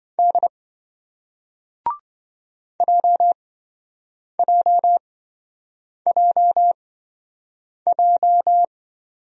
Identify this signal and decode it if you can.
Morse code, heard as a keyed tone, about 26 wpm, spaced as for 7 wpm, 700 Hz. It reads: D JJJJ